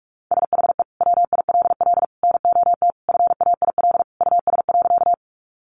Morse code, heard as a keyed tone, about 44 wpm, 700 Hz. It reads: SHE WILL NOT FAIL US.
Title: Sentences from Top 1100 Words 45wpm